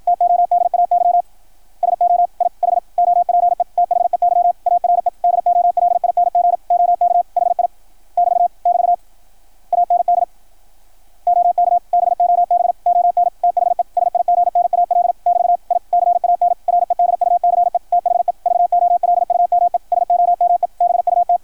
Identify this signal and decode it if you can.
Morse code, heard as a keyed tone, about 53 wpm, 700 Hz, with radio noise. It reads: TODAY SO I HOPE THEY ARE DOLING OK HI <BT> <BT> AND OK BOB ON THE SIGNAL <BT> I CAN REDUCE THE VOLUME SOME BUT
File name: archived / CRSnet